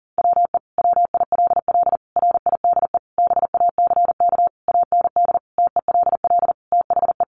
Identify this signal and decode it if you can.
Morse code, heard as a keyed tone, about 40 wpm, 700 Hz. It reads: WE WILL RIDE BACK AND TELL THE
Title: Sentences from Top 1200 Words 40wpm